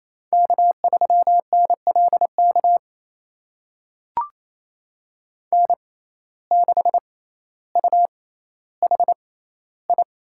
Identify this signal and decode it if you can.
Morse code, heard as a keyed tone, about 28 wpm, spaced as for 14 wpm, 700 Hz. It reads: K3NLK N6UHI
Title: USA Call Signs - 6x Character Spacing - 28wpm